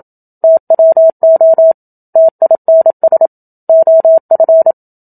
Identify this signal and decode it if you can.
Morse code, heard as a keyed tone, about 27 wpm, 650 Hz. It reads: TWO TINS OF
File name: Book Two / 01-Chapter 1 - 27 WPM